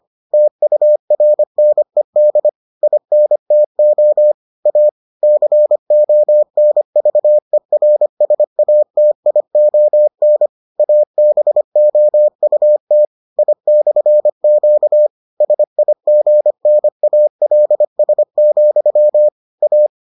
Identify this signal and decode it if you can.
Morse code, heard as a keyed tone, about 25 wpm, 600 Hz. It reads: TURNED INTO A CONVERSATION ABOUT I/Q SIGNALS, A